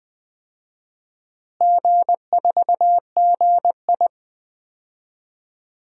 Morse code, heard as a keyed tone, about 20 wpm, 700 Hz. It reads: G4GI
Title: DXCC-100 Call Signs - 4 Characters - Repeated 3x - 20wpm